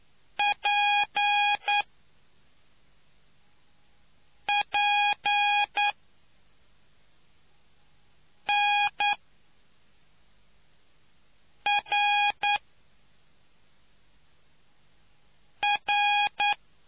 Morse code, heard as a keyed tone, about 9 wpm, spaced as for 3 wpm, 820 Hz. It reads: PPNRR